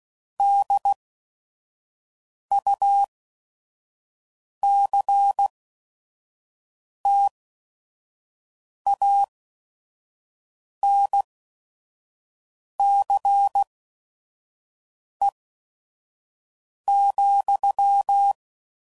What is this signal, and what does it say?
Morse code, heard as a keyed tone, about 16 wpm, spaced as for 5 wpm, 800 Hz. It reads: DUCTANCE,
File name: CW Code Groups @5WPM / Technical Subjects / XL